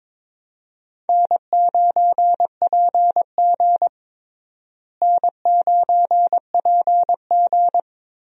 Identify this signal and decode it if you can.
Morse code, heard as a keyed tone, about 22 wpm, 700 Hz. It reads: N9PG N9PG
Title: USA Call Signs - 4 Characters - Repeated 2x - 22wpm